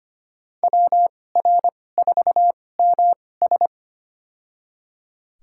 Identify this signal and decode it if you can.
Morse code, heard as a keyed tone, about 25 wpm, spaced as for 18 wpm, 700 Hz. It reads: WR4MS
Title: USA Call Signs - 5 Characters - 2x Character Spacing - 25wpm